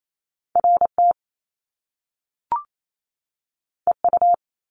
Morse code, heard as a keyed tone, about 28 wpm, 700 Hz. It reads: R T E U
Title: Top Two Letter Combinations in Words 28wpm